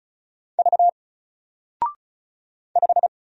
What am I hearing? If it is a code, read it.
Morse code, heard as a keyed tone, about 34 wpm, 700 Hz. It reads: V 5
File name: Single-Letter-Number-35wpm